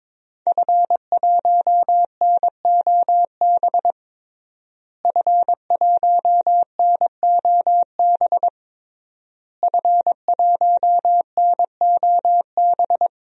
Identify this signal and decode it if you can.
Morse code, heard as a keyed tone, about 22 wpm, 700 Hz. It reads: F1NOB F1NOB F1NOB
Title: DXCC-100 Call Signs - 5 Characters - Repeated 3x - 22wpm